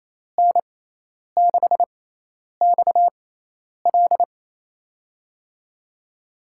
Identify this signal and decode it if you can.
Morse code, heard as a keyed tone, about 27 wpm, spaced as for 10 wpm, 700 Hz. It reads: N6XL
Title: USA Call Signs - 5 Characters - 6x Character Spacing - 28wpm